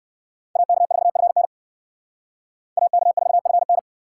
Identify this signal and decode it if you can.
Morse code, heard as a keyed tone, about 61 wpm, 700 Hz. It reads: AK4FN AK4FN